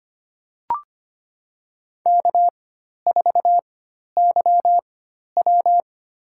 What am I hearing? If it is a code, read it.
Morse code, heard as a keyed tone, about 25 wpm, spaced as for 12 wpm, 690 Hz. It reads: K4YW
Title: USA Call Signs - 4 Characters - 4x Character Spacing - 25wpm